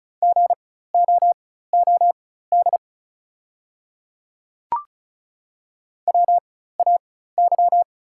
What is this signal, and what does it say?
Morse code, heard as a keyed tone, about 36 wpm, spaced as for 16 wpm, 700 Hz. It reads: GOOD WAY
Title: 4x Character Spacing - Top 200 Words - 35wpm